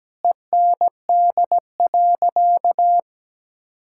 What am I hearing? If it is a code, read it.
Morse code, heard as a keyed tone, about 17 wpm, 700 Hz. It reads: END.
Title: Sentences from Top 1000 Words - Review Part 5 - 17wpm